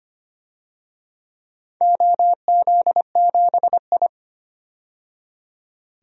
Morse code, heard as a keyed tone, about 25 wpm, 700 Hz. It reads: OZ7I
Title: DXCC-100 Call Signs - 4 Characters - 25wpm